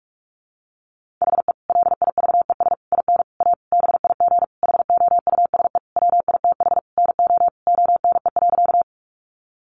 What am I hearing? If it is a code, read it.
Morse code, heard as a keyed tone, about 44 wpm, 700 Hz. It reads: HE LIVES IN A BIG HOUSE WITH NO ONE.
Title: Additional Easy Phrases 45wpm